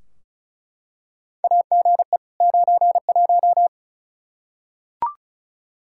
Morse code, heard as a keyed tone, about 36 wpm, 700 Hz. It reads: AGE 91
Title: Practice QSO - Exchange 3 - More Info - 35wpm